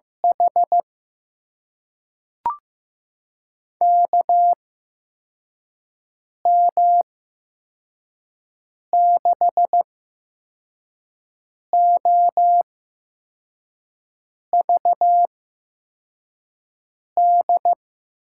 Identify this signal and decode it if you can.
Morse code, heard as a keyed tone, about 16 wpm, spaced as for 4 wpm, 700 Hz. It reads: H KM6OVD
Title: USA Call Signs - 6 Characters - 8x Character Spacing - 15wpm